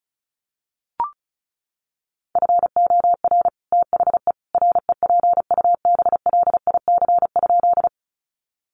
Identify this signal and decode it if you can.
Morse code, heard as a keyed tone, about 34 wpm, 700 Hz. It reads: FOR THE REPUBLIC?
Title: Sentences from Top 1500 Words 35wpm